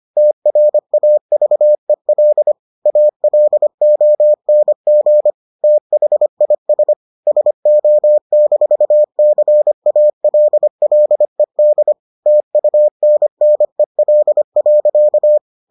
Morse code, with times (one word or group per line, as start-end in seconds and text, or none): 0.2-2.5 TRAVEL
2.9-5.3 ALONG
5.6-6.9 THIS
7.3-11.9 SO-CALLED
12.3-15.4 TUNNEL.